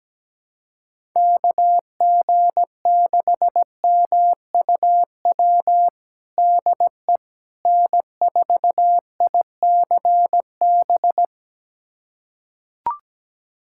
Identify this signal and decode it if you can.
Morse code, heard as a keyed tone, about 17 wpm, 700 Hz. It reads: KG6MUW DE N4ICB
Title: USA Callsign DE Callsign 17wpm